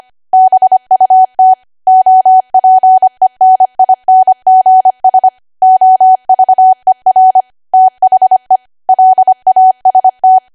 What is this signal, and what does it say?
Morse code, heard as a keyed tone, about 25 wpm, 750 Hz. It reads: BUT OPENINGS OVER THE LAST
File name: cwa / W1AW-Level3 / 25WPM